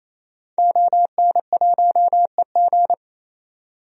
Morse code, heard as a keyed tone, about 28 wpm, 700 Hz. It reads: ON1EG
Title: DXCC-100 Call Signs - 5 Characters - 28wpm